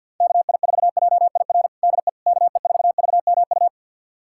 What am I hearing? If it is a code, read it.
Morse code, heard as a keyed tone, about 53 wpm, 700 Hz. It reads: KI4JIR DE KE4VMU
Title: USA Callsign DE Callsign - 6 Characters - 50wpm